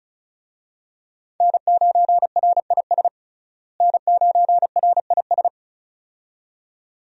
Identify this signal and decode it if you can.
Morse code, heard as a keyed tone, about 34 wpm, 700 Hz. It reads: N9RIS N9RIS
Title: USA Call Signs - 5 Characters - Repeated 2x - 35wpm